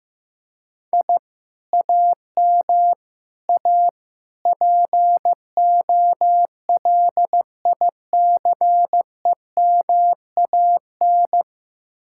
Morse code, heard as a keyed tone, about 15 wpm, 700 Hz. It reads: I AM A POLICEMAN
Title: Speed-Racing Common QSO Phrases Encore - 15wpm